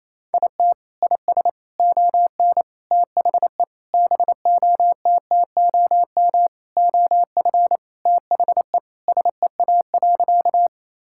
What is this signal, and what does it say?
Morse code, heard as a keyed tone, about 28 wpm, 700 Hz. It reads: IT IS ON THE BOTTOM OF THE SEA.
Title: Sentences from Top 1000 Words - Review Part 6 - 28wpm